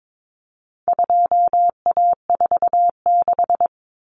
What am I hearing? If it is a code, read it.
Morse code, heard as a keyed tone, about 22 wpm, 700 Hz. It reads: 2A46
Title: Common Cut Numbers - 4-Digit 22wpm